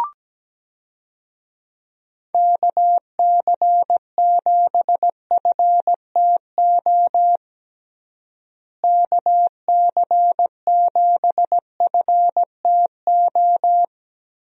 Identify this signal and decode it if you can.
Morse code, heard as a keyed tone, about 17 wpm, 700 Hz. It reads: KC7FTO KC7FTO
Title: USA Call Signs - 6 Characters - Repeated 2x - 17wpm